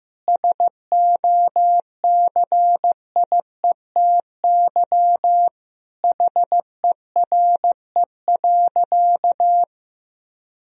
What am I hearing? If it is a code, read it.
Morse code, heard as a keyed tone, about 14 wpm, 700 Hz. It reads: SOCIETY HERE.
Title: Sentences from Top 1000 Words - Review Part 4 - 15wpm